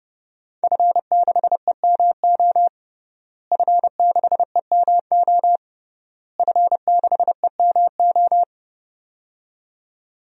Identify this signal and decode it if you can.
Morse code, heard as a keyed tone, about 32 wpm, 700 Hz. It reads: F6EMO F6EMO F6EMO
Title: DXCC-100 Call Signs - 5 Characters - Repeated 3x - 30wpm